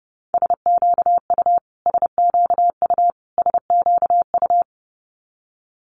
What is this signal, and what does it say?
Morse code, heard as a keyed tone, about 30 wpm, 700 Hz. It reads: SQU SQU SQU
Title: Multi-letter Phonemes Repeated 3x 30wpm